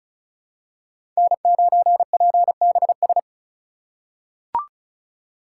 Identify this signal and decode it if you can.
Morse code, heard as a keyed tone, about 34 wpm, 700 Hz. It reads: N9PBS
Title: USA Call Signs - 5 Characters - 2x Character Spacing - 35wpm